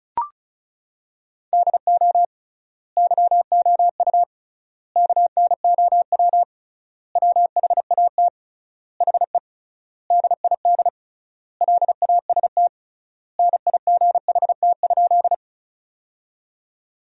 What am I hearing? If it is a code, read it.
Morse code, heard as a keyed tone, about 36 wpm, 700 Hz. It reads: DO YOU KNOW WHAT HE DID LAST NIGHT?
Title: 3x Word-Spacing - Sentences from Top 300 Words - 35wpm